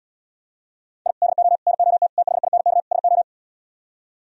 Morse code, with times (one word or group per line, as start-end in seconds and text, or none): 1.1-1.1 I
1.2-1.6 DO
1.7-2.1 NOT
2.2-2.8 THINK
2.9-3.2 SO